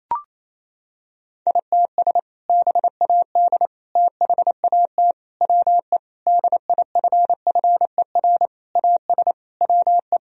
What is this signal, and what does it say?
Morse code, heard as a keyed tone, about 28 wpm, 700 Hz. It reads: ITS BAD THAT WE DIFFER AS WE
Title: Sentences from Top 1000 Words - Review Part 1 - 28wpm